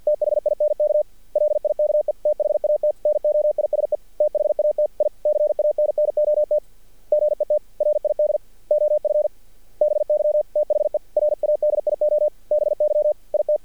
Morse code, heard as a keyed tone, about 54 wpm, 600 Hz. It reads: THINK LIKE THAT NOISE THAT I CANNOT GET RID OF BY THE RADIO BY IT
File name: archived / CRSnet